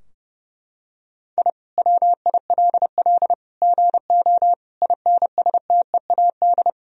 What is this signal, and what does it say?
Morse code, heard as a keyed tone, about 30 wpm, 700 Hz. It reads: I WILL GO INSTEAD
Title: Sentences from Top 1000 Words - Review Part 4 - 30wpm